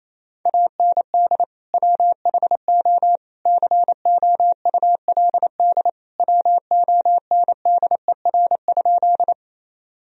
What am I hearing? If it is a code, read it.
Morse code, heard as a keyed tone, about 28 wpm, 700 Hz. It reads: AND WHO COULD WONDER?
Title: Sentences from Top 1000 Words - Review Part 3 - 28wpm